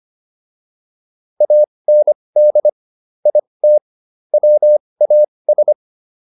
Morse code, heard as a keyed tone, about 25 wpm, spaced as for 20 wpm, 600 Hz. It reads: AND IT WAS